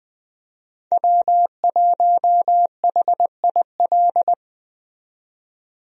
Morse code, heard as a keyed tone, about 21 wpm, 700 Hz. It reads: W1HIL
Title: USA Call Signs - 5 Characters - 20wpm